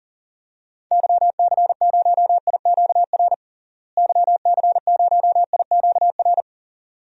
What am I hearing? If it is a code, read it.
Morse code, heard as a keyed tone, about 40 wpm, 700 Hz. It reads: YC0IQR YC0IQR
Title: DXCC-100 Call Signs - 6 Characters - Repeated 2x - 40wpm